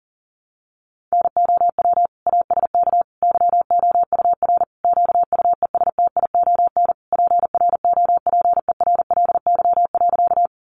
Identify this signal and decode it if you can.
Morse code, heard as a keyed tone, about 40 wpm, 700 Hz. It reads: NOW ASK YOUR QUESTION PROPERLY.
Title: Sentences from Top 1100 Words 40wpm